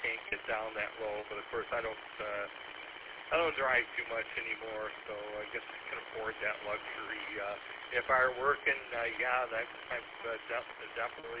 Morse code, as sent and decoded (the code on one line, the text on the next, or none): none